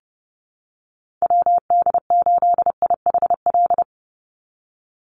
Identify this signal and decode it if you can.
Morse code, heard as a keyed tone, about 30 wpm, 700 Hz. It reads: WD8IHL
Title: USA Call Signs 30wpm